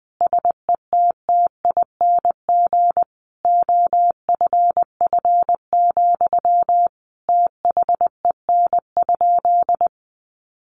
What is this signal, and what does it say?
Morse code, heard as a keyed tone, about 20 wpm, 700 Hz. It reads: SETTING OFF, THEN?
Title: Sentences from Top 1300 Words 20wpm